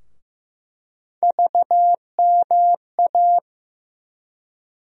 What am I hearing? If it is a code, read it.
Morse code, heard as a keyed tone, about 15 wpm, 700 Hz. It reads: VMA
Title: USA Call Sign Postfixes - 3 Letters 15wpm